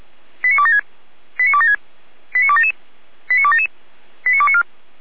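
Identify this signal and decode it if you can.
Five-tone selective call (ZVEI), heard as quick runs of five tones, tones of about 70 ms. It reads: 89276 89276 89277 89277 89294
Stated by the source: Received by SDR, ZVEI five-tone selective call.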